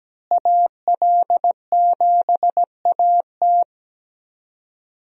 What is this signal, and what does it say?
Morse code, heard as a keyed tone, about 17 wpm, 700 Hz. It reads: AL7AT
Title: USA Call Signs - 5 Characters - Repeated 3x - 17wpm